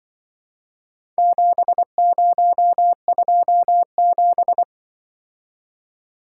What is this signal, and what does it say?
Morse code, heard as a keyed tone, about 24 wpm, 700 Hz. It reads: 7027
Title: Speed-Racing - Four Letter-Number Combinations - 20wpm